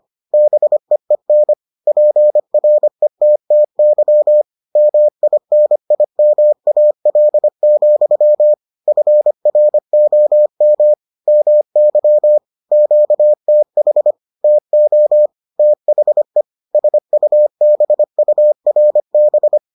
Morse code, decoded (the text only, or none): BEEN PRETTY MINIMAL, FROM MY QTH TO THE SUBURB